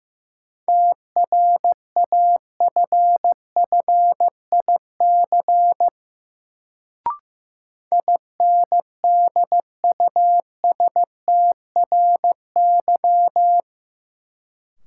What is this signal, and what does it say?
Morse code, heard as a keyed tone, about 15 wpm, 700 Hz. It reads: TRAFFIC INDUSTRY
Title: Sentences from Top 1300 Words - New Word Drill Alt - 15wpm